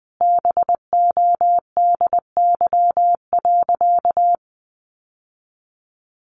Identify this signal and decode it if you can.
Morse code, heard as a keyed tone, about 20 wpm, 700 Hz. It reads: BODY.
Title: Sentences from Top 1500 Words 20wpm